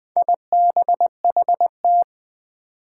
Morse code, heard as a keyed tone, about 20 wpm, 700 Hz. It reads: IBHT
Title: Four Challenging Letter-Number Combinations 20wpm